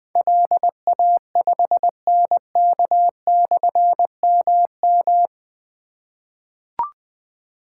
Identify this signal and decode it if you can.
Morse code, heard as a keyed tone, about 20 wpm, 700 Hz. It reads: LA5NK/MM
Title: Difficult Call Signs 20wpm